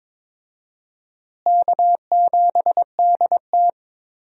Morse code, heard as a keyed tone, about 22 wpm, 700 Hz. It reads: K7DT
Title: USA Call Signs - 4 Characters - Repeated 2x - 22wpm